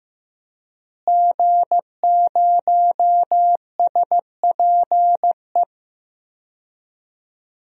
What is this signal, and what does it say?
Morse code, heard as a keyed tone, about 16 wpm, 700 Hz. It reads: G0SPE
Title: DXCC-100 Call Signs - 5 Characters - Repeated 3x - 15wpm